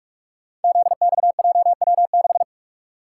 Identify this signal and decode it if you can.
Morse code, heard as a keyed tone, about 44 wpm, 700 Hz. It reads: ZXJW6
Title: Code Groups - 5 Letter:Numbers - 45wpm